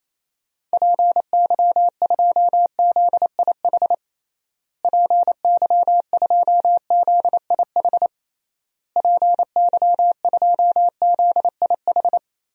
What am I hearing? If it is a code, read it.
Morse code, heard as a keyed tone, about 28 wpm, 700 Hz. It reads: PY2ZIH PY2ZIH PY2ZIH
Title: DXCC-100 Call Signs - 6 Characters - Repeated 3x - 28wpm